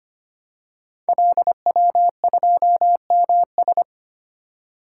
Morse code, heard as a keyed tone, about 25 wpm, 700 Hz. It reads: LW2MS
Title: Code Groups - 5 Letter:Numbers - 25wpm